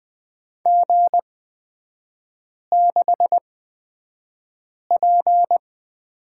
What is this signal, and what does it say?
Morse code, heard as a keyed tone, about 20 wpm, spaced as for 5 wpm, 700 Hz. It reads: G6P
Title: 255 Lesson 41e - Period - Three Characters 20wpm